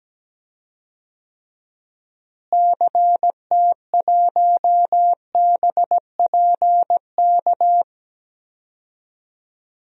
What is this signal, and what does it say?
Morse code, heard as a keyed tone, about 17 wpm, 700 Hz. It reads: CT1BPK
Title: DXCC-100 Call Signs - 6 Characters - Repeated 3x - 17wpm